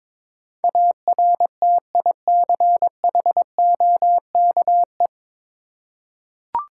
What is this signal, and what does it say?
Morse code, heard as a keyed tone, about 22 wpm, 690 Hz. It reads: ARTICHOKE
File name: Difficult Words with Hints - 22wpm